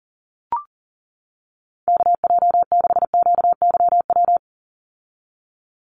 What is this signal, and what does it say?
Morse code, heard as a keyed tone, about 40 wpm, 700 Hz. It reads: KJ6QYW
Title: USA Call Signs 40wpm